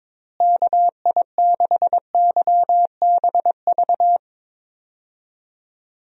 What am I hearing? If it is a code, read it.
Morse code, heard as a keyed tone, about 22 wpm, 700 Hz. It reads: KI6YBV
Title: USA Call Signs - 6 Characters - Repeated 3x - 22wpm